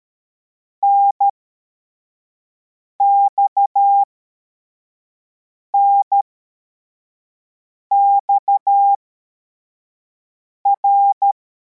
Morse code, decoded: NXNXR